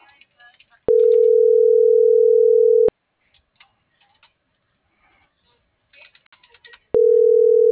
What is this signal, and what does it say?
Ringback tone (North America), 440 + 480 Hz.